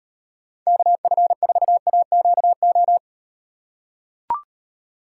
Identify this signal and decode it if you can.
Morse code, heard as a keyed tone, about 38 wpm, 700 Hz. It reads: KF4AQO